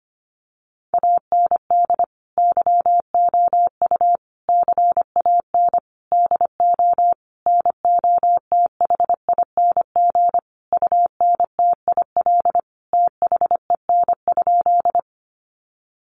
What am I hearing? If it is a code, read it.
Morse code, heard as a keyed tone, about 25 wpm, 700 Hz. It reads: AND YOU CAN DO NOTHING UNTIL THEN?
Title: Sentences from Top 500 Words 25wpm